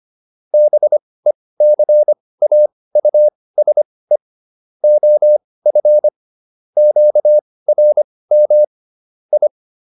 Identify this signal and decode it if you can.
Morse code, heard as a keyed tone, about 25 wpm, spaced as for 18 wpm, 600 Hz. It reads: BECAUSE OF QRM I
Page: intermediate-practice-files